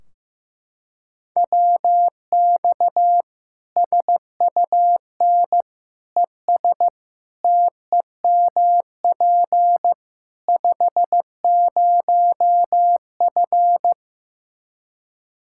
Morse code, heard as a keyed tone, about 15 wpm, 700 Hz. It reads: WX SUN ES TEMP 50F
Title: Practice QSO - Exchange 2 - Info - 15wpm